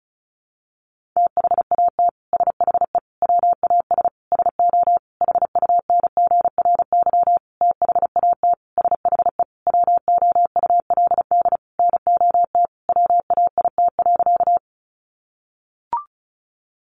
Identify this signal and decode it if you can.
Morse code, heard as a keyed tone, about 34 wpm, 700 Hz. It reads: THAT SHE WAS SO HUNGRY THAT SHE WOULD NOT WAIT.
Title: Sentences from Top 1400 Words 35wpm